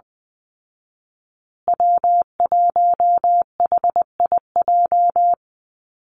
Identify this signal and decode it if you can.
Morse code, heard as a keyed tone, about 20 wpm, 700 Hz. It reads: W1HIJ
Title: Call Signs Repeated 2x - 20wpm